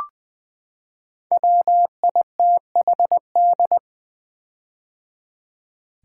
Morse code, heard as a keyed tone, about 20 wpm, 700 Hz. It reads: WITHD
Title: Build A Word - 5 Letters At A Time - 10-Letter Words - 20wpm